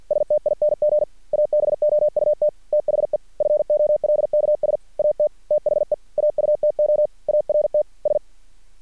Morse code, heard as a keyed tone, about 50 wpm, 600 Hz. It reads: STING ABOUT THE FOLKS AT THE AUTO ART S
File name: archived / CRSnet